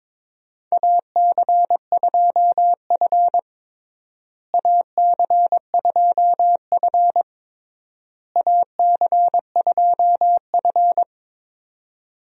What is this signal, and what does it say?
Morse code, heard as a keyed tone, about 22 wpm, 700 Hz. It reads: AC2F AC2F AC2F
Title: USA Call Signs - 4 Characters - Repeated 3x - 22wpm